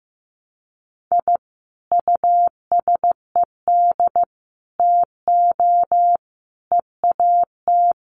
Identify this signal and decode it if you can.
Morse code, heard as a keyed tone, about 15 wpm, 700 Hz. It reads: I USED TO EAT
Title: Sentences from Top 1200 Words 15wpm